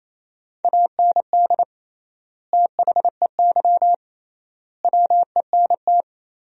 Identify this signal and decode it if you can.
Morse code, heard as a keyed tone, about 28 wpm, 700 Hz. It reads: AND THEY WENT
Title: 3x Word-Spacing - Sentences from Top 200 Words - 28wpm